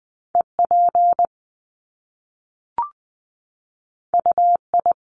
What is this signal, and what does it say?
Morse code, heard as a keyed tone, about 20 wpm, 700 Hz. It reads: EP UI
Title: Top Two Letter Combinations in Words 20wpm